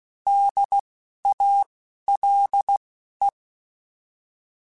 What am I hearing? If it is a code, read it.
Morse code, heard as a keyed tone, about 16 wpm, spaced as for 12 wpm, 800 Hz. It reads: DALE
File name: Alphabetical / Alphabetical mp3s / Ds @10